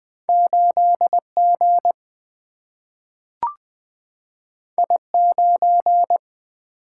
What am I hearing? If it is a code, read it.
Morse code, heard as a keyed tone, about 20 wpm, 700 Hz. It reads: 8G I9
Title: Single-Letter Single-Number 20wpm